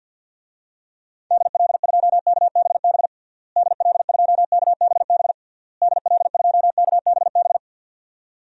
Morse code, heard as a keyed tone, about 47 wpm, 700 Hz. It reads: DL2KBB DL2KBB DL2KBB
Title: DXCC-100 Call Signs - 6 Characters - Repeated 3x - 50wpm